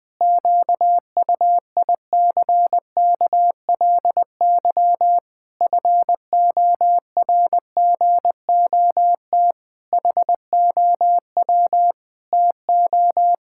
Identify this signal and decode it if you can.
Morse code, heard as a keyed tone, about 20 wpm, 700 Hz. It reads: QUICKLY FORGOT HOW TO
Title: Pangram Challenge v2 - 20wpm